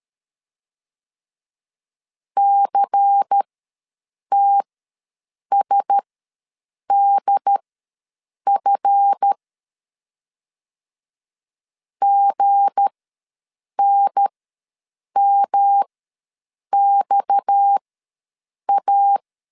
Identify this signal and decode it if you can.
Morse code, heard as a keyed tone, about 13 wpm, spaced as for 7 wpm, 800 Hz. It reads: CTSDF GNMXA
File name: LEZIONI / 9 2 ws 12wpm 3406